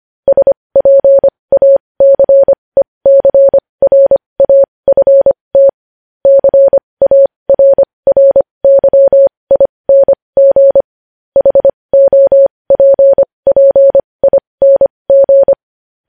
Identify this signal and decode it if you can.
Morse code, heard as a keyed tone, about 25 wpm, spaced as for 20 wpm, 550 Hz. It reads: SPACECRAFT CARRYING HOPPING
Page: News Headlines in Morse Code at 20 WPM